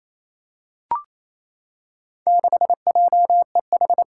Morse code, heard as a keyed tone, about 28 wpm, 700 Hz. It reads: 6JEH
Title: Four Challenging Letter-Number Combinations 28wpm